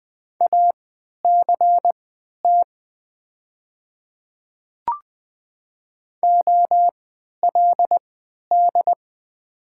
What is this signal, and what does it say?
Morse code, heard as a keyed tone, about 20 wpm, spaced as for 11 wpm, 700 Hz. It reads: ACT OLD
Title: 3x Character Spacing - Top 200 Words - 20wpm